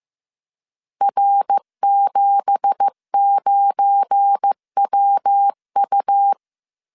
Morse code, heard as a keyed tone, about 15 wpm, 800 Hz. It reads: R79WU
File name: CW / DET / 1 9 1500C S C 14wpm 3245